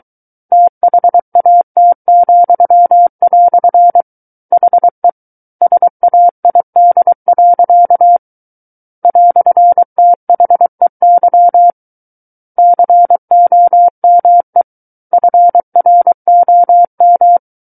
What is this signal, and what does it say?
Morse code, heard as a keyed tone, about 23 wpm, 700 Hz. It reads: THAT," HE SAID. "THEY COME FROM
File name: Book One / 14-Chapter 14 - 23 WPM